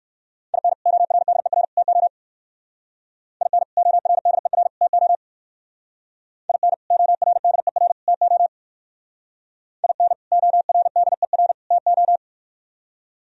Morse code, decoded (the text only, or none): IN ORDER TO IN ORDER TO IN ORDER TO IN ORDER TO